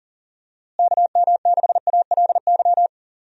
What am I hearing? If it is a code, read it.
Morse code, heard as a keyed tone, about 40 wpm, 700 Hz. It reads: KM6ALY
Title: USA Call Signs - 6 Characters - Repeated 2x - 40wpm